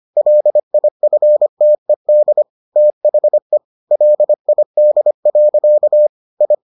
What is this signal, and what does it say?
Morse code, heard as a keyed tone, about 25 wpm, 600 Hz. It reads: LIFTED THE LID. I